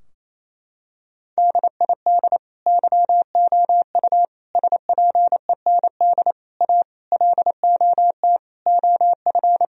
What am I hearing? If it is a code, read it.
Morse code, heard as a keyed tone, about 27 wpm, 700 Hz. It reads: DID YOU SPEND A LOT OF
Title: Sentences from Top 1000 Words - Review Part 4 - 28wpm